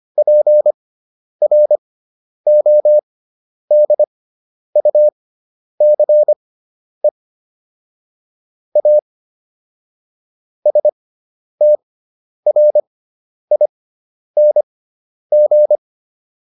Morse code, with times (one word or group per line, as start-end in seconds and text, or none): 0.2-7.1 PRODUCE
8.8-9.0 A
10.7-15.8 STRING